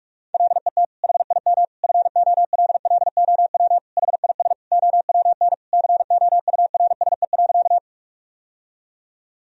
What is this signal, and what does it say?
Morse code, heard as a keyed tone, about 46 wpm, 700 Hz. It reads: LET HIM FOLLOW HIS OWN COURSE.
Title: Sentences from Top 1000 Words - Review Part 1 - 45wpm